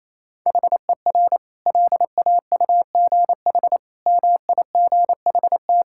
Morse code, heard as a keyed tone, about 27 wpm, 700 Hz. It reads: HER LAUGH MIGHT
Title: Sentences from Top 1000 Words - Review Part 5 - 28wpm